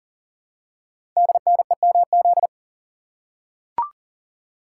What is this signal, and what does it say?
Morse code, heard as a keyed tone, about 42 wpm, 690 Hz. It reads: DNEMZ
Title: Code Groups - 5 Letters - 40wpm